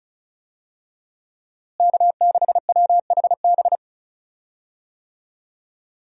Morse code, heard as a keyed tone, about 34 wpm, 700 Hz. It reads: K6WHB
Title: USA Call Signs - 5 Characters - Repeated 2x - 35wpm